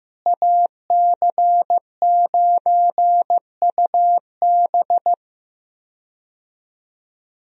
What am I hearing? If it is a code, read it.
Morse code, heard as a keyed tone, about 15 wpm, 700 Hz. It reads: AC9UB
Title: USA Call Signs - 5 Characters - Repeated 2x - 15wpm